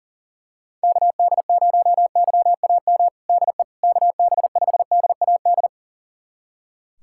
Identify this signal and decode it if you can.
Morse code, heard as a keyed tone, about 40 wpm, 690 Hz. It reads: KD0YAM DE KB5DAD
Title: USA Callsign DE Callsign - 6 Characters - 40wpm